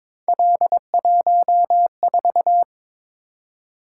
Morse code, heard as a keyed tone, about 22 wpm, 700 Hz. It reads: L14
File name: practice / 22wpm / Three Letter-Number Combinations - 22wpm